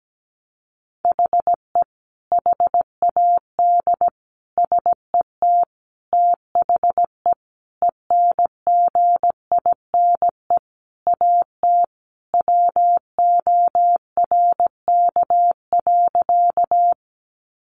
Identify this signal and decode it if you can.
Morse code, heard as a keyed tone, about 17 wpm, 700 Hz. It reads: HE HAD SET THE ENGINE AT WORK.
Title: Sentences from Top 1500 Words 17wpm